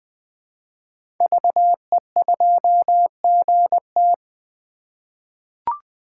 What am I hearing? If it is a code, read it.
Morse code, heard as a keyed tone, about 20 wpm, 700 Hz. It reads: VE2GT
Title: 2023 SOTA Chaser Call Signs - 20wpm